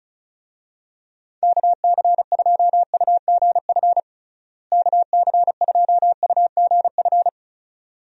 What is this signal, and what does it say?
Morse code, heard as a keyed tone, about 36 wpm, 700 Hz. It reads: KC2UGF KC2UGF
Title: USA Call Signs - 6 Characters - Repeated 2x - 35wpm